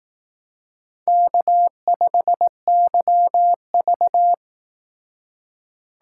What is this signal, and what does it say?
Morse code, heard as a keyed tone, about 18 wpm, 700 Hz. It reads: K5YV
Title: Speed Racing - USA 1x2 Call Signs 15wpm